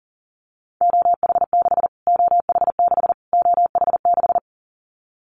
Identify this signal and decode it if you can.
Morse code, heard as a keyed tone, about 40 wpm, 700 Hz. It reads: OH6 OH6 OH6
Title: Callsign Prefixes for Top 25 DXCC - Repeated 3x - 40wpm